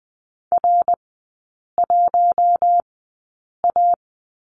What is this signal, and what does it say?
Morse code, heard as a keyed tone, about 20 wpm, spaced as for 8 wpm, 700 Hz. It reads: R1A
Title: 100 Lesson 17e - Letter u - Three Characters 20wpm